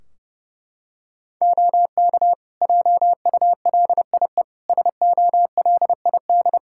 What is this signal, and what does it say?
Morse code, heard as a keyed tone, about 32 wpm, 700 Hz. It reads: OK JULIE SOLID
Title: Practice QSO - Exchange 3 - More Info - 30wpm